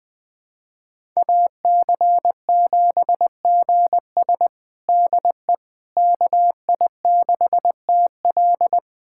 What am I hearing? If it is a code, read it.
Morse code, heard as a keyed tone, about 21 wpm, 700 Hz. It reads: AC7GS DE KI6TL
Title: USA Callsign DE Callsign - 5 Characters - 20wpm